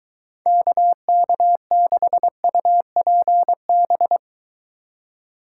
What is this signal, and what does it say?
Morse code, heard as a keyed tone, about 23 wpm, 700 Hz. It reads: KK6UPB